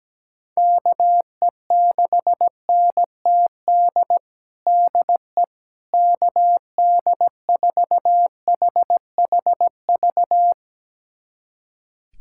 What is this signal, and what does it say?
Morse code, heard as a keyed tone, about 17 wpm, 700 Hz. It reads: KE6NTD DE KD4HHV
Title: USA Callsign DE Callsign 17wpm